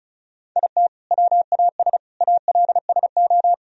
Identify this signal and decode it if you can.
Morse code, heard as a keyed tone, about 36 wpm, 700 Hz. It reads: IT WAS ALSO
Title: Sentences from Top 1000 Words - Review Part 1 - 35wpm